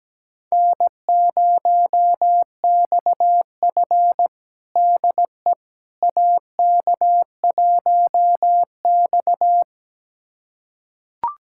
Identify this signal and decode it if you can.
Morse code, heard as a keyed tone, about 17 wpm, 700 Hz. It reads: N0XF DE AK1X
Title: USA Callsign DE Callsign - 4 Characters - 17wpm